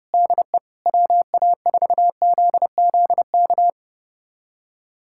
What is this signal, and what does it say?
Morse code, heard as a keyed tone, about 30 wpm, 700 Hz. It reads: DE WA4ZZK
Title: USA Callsign DE Callsign - 6 Characters - 30wpm